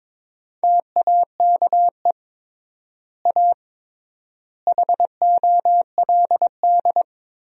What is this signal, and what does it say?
Morse code, heard as a keyed tone, about 22 wpm, 700 Hz. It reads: TAKE A HOLD
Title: 3x Word-Spacing - Sentences from Top 400 Words - 22wpm